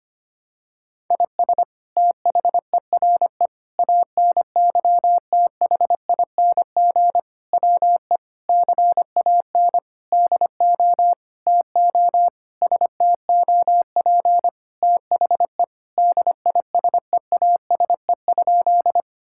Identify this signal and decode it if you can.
Morse code, heard as a keyed tone, about 25 wpm, 700 Hz. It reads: IS THERE ANYTHING WE CAN DO TO STOP THE DISEASE?
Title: Sentences from Top 1000 Words - Review Part 6 - 25wpm